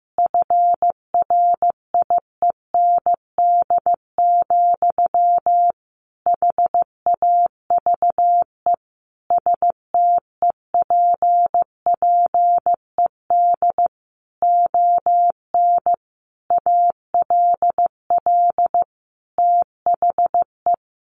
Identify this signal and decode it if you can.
Morse code, heard as a keyed tone, about 15 wpm, 700 Hz. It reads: FRIEND, HAVE STEPPED ON ALL THE
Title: A Sound Of Thunder 15wpm